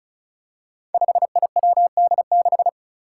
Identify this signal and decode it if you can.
Morse code, heard as a keyed tone, about 36 wpm, 700 Hz. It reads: 5IWD6
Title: Five Challenging Letter-Number Combinations 35wpm